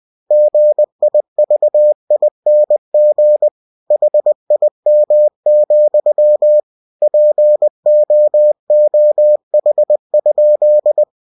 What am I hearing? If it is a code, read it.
Morse code, heard as a keyed tone, about 20 wpm, 600 Hz. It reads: GIVING HIM, POOH?
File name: cw-books / winnie-the-pooh / winnie-20wpm-page-0051